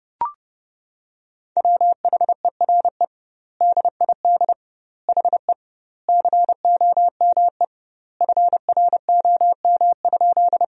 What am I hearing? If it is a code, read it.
Morse code, heard as a keyed tone, about 30 wpm, 700 Hz. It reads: WHERE DID HE COME FROM?
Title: 2x Word-Spacing - Sentences from Top 200 Words - 30wpm